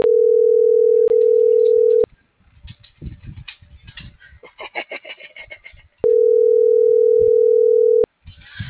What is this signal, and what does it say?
Ringback tone (North America), 440 + 480 Hz.